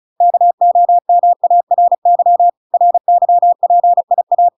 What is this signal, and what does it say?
Morse code, heard as a keyed tone, about 35 wpm, 700 Hz. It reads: KOMARY RYPIA